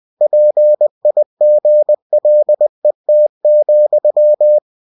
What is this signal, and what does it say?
Morse code, heard as a keyed tone, about 20 wpm, 600 Hz. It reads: PIGLET,